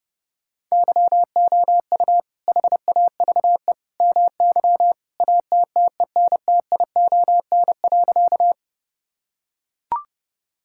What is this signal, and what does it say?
Morse code, heard as a keyed tone, about 30 wpm, 700 Hz. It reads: YOU HAVE MY ATTENTION.
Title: Sentences from Top 1000 Words - Review Part 6 - 30wpm